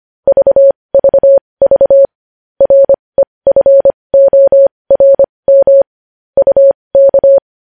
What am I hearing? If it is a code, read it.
Morse code, heard as a keyed tone, about 25 wpm, spaced as for 20 wpm, 550 Hz. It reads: VVV REFORM UK
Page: news-headlines-in-morse-code-at-20-wpm